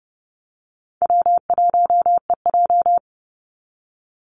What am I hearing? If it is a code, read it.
Morse code, heard as a keyed tone, about 30 wpm, 700 Hz. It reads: W1EJ
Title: NA SOTA Activator Callsigns 30wpm